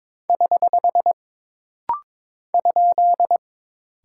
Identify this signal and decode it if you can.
Morse code, heard as a keyed tone, about 22 wpm, 700 Hz. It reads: <HH> ?
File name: practice / 22wpm / Single Punctuation-Prosign - Rapid-Fire - 22wpm